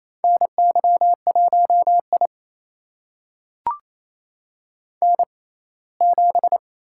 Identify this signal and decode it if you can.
Morse code, heard as a keyed tone, about 28 wpm, spaced as for 19 wpm, 700 Hz. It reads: NY1I N7
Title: USA Call Signs - 4 Characters - 6x Character Spacing - 28wpm